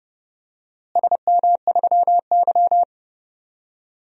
Morse code, heard as a keyed tone, about 30 wpm, 700 Hz. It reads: SM3Y
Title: DXCC-100 Call Signs - 4 Characters - Repeated 2x - 30wpm